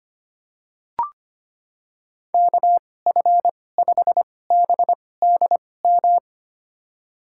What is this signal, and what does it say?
Morse code, heard as a keyed tone, about 25 wpm, spaced as for 18 wpm, 700 Hz. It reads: KF5BDM
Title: USA Call Signs - 6 Characters - 2x Character Spacing - 25wpm